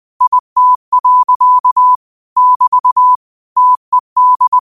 Morse code, heard as a keyed tone, about 20 wpm, 1000 Hz. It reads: IT. <BT> TED
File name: qotdfiles / 20wpm